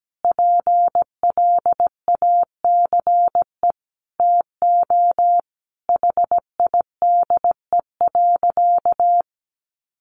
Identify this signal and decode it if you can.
Morse code, heard as a keyed tone, about 17 wpm, 700 Hz. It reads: PLACE TO HIDE.
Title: Easy Phrases 17wpm